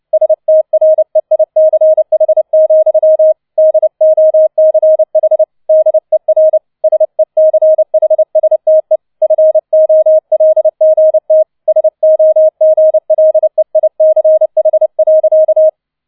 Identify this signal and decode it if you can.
Morse code, heard as a keyed tone, about 29 wpm, 630 Hz. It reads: STREICH, DOCH DER SECHSTE FOLGT SOGLEICH.